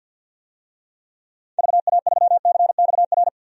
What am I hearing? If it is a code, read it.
Morse code, heard as a keyed tone, about 53 wpm, 700 Hz. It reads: VA3CXR
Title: DXCC-100 Call Signs - 6 Characters - 50wpm